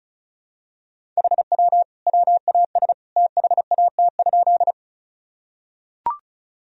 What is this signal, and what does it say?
Morse code, heard as a keyed tone, about 34 wpm, 700 Hz. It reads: HW WAS THAT?
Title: Sentences from Top 100 Words and Abbreviations 35wpm